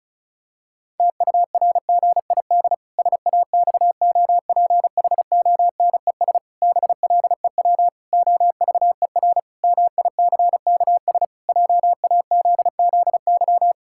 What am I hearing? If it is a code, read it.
Morse code, heard as a keyed tone, about 34 wpm, 700 Hz. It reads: TURGID SAXOPHONES BLEW OVER MICKS JAZZY
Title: Pangram Challenge v2 - 35wpm